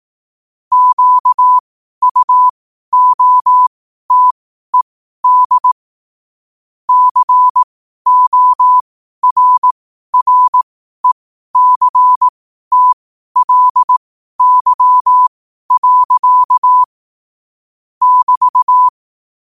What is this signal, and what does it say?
Morse code, heard as a keyed tone, about 18 wpm, spaced as for 12 wpm, 1000 Hz. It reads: QUOTED CORRECTLY. <BT>
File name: qotdfiles / 12wpm